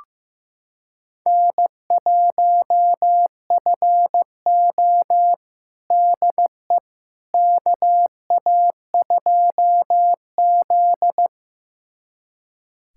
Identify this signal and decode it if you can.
Morse code, heard as a keyed tone, about 15 wpm, 700 Hz. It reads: N1FO DE KA2Z
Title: USA Callsign DE Callsign - 4 Characters - 15wpm